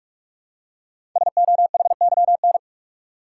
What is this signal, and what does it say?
Morse code, heard as a keyed tone, about 44 wpm, 700 Hz. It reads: SOHYN
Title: Code Groups - 5 Letters - 45wpm